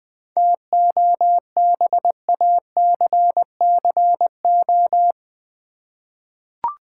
Morse code, heard as a keyed tone, about 20 wpm, 700 Hz. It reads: TOBACCO
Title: 7 Letter Words - First 3 Letters Given - 20wpm